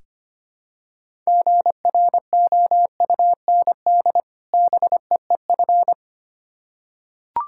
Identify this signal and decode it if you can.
Morse code, heard as a keyed tone, about 25 wpm, 700 Hz. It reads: GROUND BEEF
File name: practice / 25wpm / Difficult Words with Hints - 25wpm